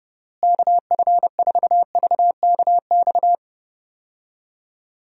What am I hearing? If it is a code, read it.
Morse code, heard as a keyed tone, about 30 wpm, 700 Hz. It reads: KF4VKX
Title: USA Call Signs - 6 Characters - Repeated 2x - 30wpm